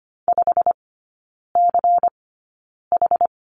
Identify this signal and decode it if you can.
Morse code, heard as a keyed tone, about 25 wpm, spaced as for 9 wpm, 700 Hz. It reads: 5CH
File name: course / 25:10wpm / 093 Lesson 16e - Letter c - Three Characters 25wpm